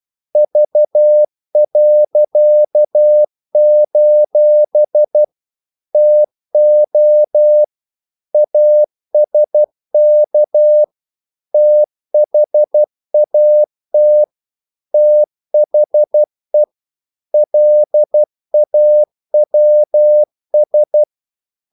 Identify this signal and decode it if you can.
Morse code, heard as a keyed tone, about 12 wpm, 600 Hz. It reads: V.: TO ASK THAT THE LAWS
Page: English quotations in CW (telegraphy - morse)